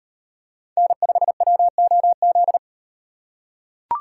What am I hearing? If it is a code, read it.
Morse code, heard as a keyed tone, about 38 wpm, 700 Hz. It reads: N5WOZ